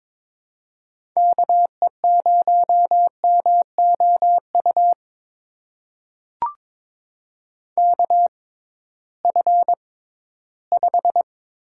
Morse code, heard as a keyed tone, about 22 wpm, spaced as for 14 wpm, 700 Hz. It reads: KE0MOU KF5
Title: USA Call Signs - 6 Characters - 6x Character Spacing - 22wpm